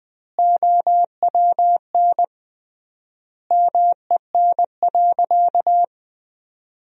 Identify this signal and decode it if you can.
Morse code, heard as a keyed tone, about 21 wpm, 700 Hz. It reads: OWN MEN.
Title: 3x Word-Spacing - Sentences from Top 300 Words - 20wpm